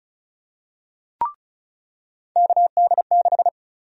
Morse code, heard as a keyed tone, about 34 wpm, 700 Hz. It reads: KD6
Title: USA Call Sign Prefixes - 35wpm